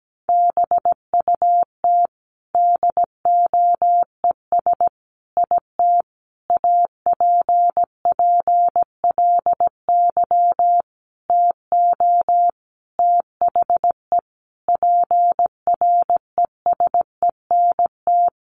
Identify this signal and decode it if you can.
Morse code, heard as a keyed tone, about 17 wpm, 700 Hz. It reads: BUT DOES IT APPLY TO THE PRESENT
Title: Sentences from Top 1200 Words 17wpm